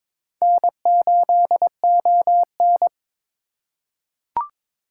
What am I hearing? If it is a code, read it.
Morse code, heard as a keyed tone, about 22 wpm, 700 Hz. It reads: N8ON
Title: USA Call Signs - 4 Characters - 4x Character Spacing - 22wpm